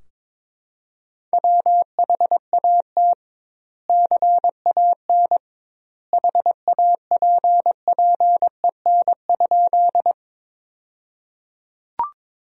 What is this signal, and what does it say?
Morse code, heard as a keyed tone, about 22 wpm, 700 Hz. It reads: WHAT CAN HAPPEN?
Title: 2x Word-Spacing - Sentences from Top 400 Words - 22wpm